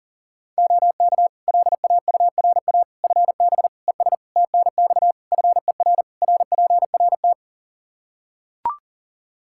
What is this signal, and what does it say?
Morse code, heard as a keyed tone, about 37 wpm, 700 Hz. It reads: OK LAURA FB ES TNX FER RPRT
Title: K1USN SST QSO Practice 40wpm